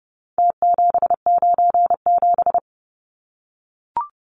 Morse code, heard as a keyed tone, about 30 wpm, 700 Hz. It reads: T797
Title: Numbers - Contest Style - 4-Digit 30wpm